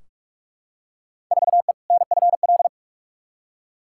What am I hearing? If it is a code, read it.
Morse code, heard as a keyed tone, about 46 wpm, 700 Hz. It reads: 4E NFL
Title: Field Day Exchanges Encore - 45wpm